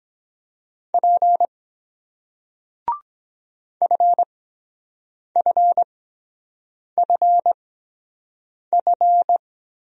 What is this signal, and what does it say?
Morse code, heard as a keyed tone, about 23 wpm, spaced as for 7 wpm, 700 Hz. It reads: P FFFF